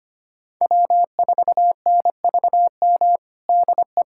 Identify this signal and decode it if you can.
Morse code, heard as a keyed tone, about 25 wpm, 700 Hz. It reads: W4NVM DE
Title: USA Callsign DE Callsign - 5 Characters - 25wpm